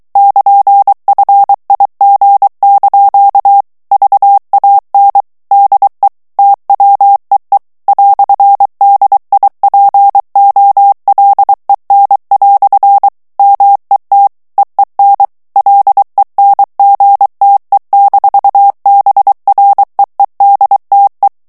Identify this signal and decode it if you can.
Morse code, heard as a keyed tone, about 23 wpm, 800 Hz. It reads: <KN>FIG) VAN DE TWEE "DIPOLEN" MET EEN LENGTE-BREEDTE
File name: HOLLANDAIS texte clair 1 vitesse 22 mots minute